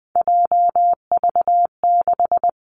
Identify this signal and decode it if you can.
Morse code, heard as a keyed tone, about 19 wpm, 700 Hz. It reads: JV6
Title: Three Challenging Letter-Number Combinations 20wpm